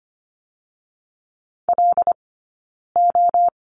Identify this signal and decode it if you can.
Morse code, heard as a keyed tone, about 25 wpm, spaced as for 9 wpm, 690 Hz. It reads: LO
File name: Lesson 15d - Number 5 - Two Characters 25wpm